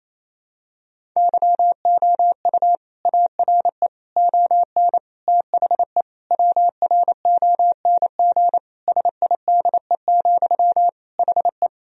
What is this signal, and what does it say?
Morse code, heard as a keyed tone, about 28 wpm, 700 Hz. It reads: YOU ARE ON THE WRONG SIDE, HE
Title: Sentences from Top 1000 Words - Review Part 1 - 28wpm